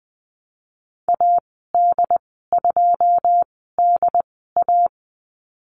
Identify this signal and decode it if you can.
Morse code, heard as a keyed tone, about 20 wpm, spaced as for 14 wpm, 700 Hz. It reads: AD2DA
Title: 082 Lesson 14g - Number 2 - Call Signs 20wpm